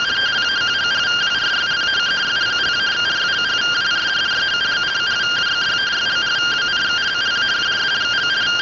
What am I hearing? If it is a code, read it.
Two-tone teleprinter signal, about 170 Hz shift.
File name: findings / MR CORONAL 55.gen-02 rtty rx